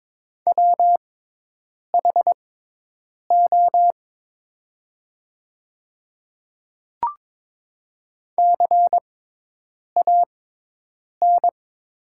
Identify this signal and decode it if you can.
Morse code, heard as a keyed tone, about 23 wpm, spaced as for 8 wpm, 700 Hz. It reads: WHO CAN
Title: 6x Character Spacing - Top 200 Words - 22wpm